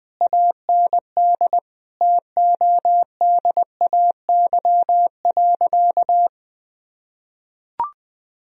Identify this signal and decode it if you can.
Morse code, heard as a keyed tone, about 20 wpm, 700 Hz. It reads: AND TODAY.
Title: Sentences from Top 1000 Words - Review Part 4 - 20wpm